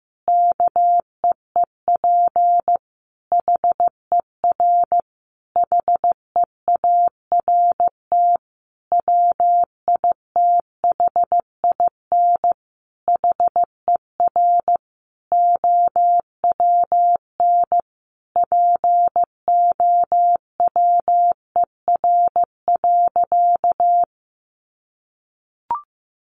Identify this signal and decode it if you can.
Morse code, heard as a keyed tone, about 15 wpm, 700 Hz. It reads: KEEP HER HEART WITHIN HER OWN POWER.
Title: Sentences from Top 1200 Words 15wpm